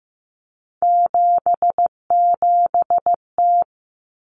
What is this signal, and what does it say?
Morse code, heard as a keyed tone, about 15 wpm, 700 Hz. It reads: 77T
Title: Common Cut Numbers - 3-Digit 15wpm